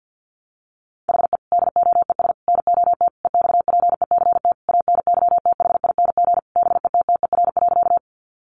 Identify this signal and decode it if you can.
Morse code, heard as a keyed tone, about 49 wpm, 700 Hz. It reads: HE DOES NOT EXPECT ANYTHING BETTER.
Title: Sentences from Top 1100 Words 50wpm